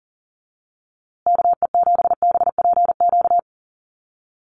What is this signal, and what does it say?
Morse code, heard as a keyed tone, about 40 wpm, 700 Hz. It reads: KE7BPQ
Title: USA Call Signs Repeated 2x 40wpm